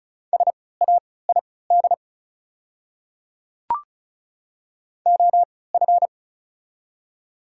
Morse code, heard as a keyed tone, about 34 wpm, spaced as for 20 wpm, 690 Hz. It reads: SAID OF
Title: 3x Character Spacing - Top 100 Words - 35wpm